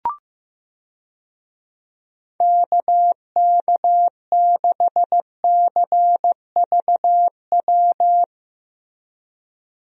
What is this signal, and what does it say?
Morse code, heard as a keyed tone, about 15 wpm, 700 Hz. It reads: KK6CVW
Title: USA Call Signs - 6 Characters - Repeated 3x - 15wpm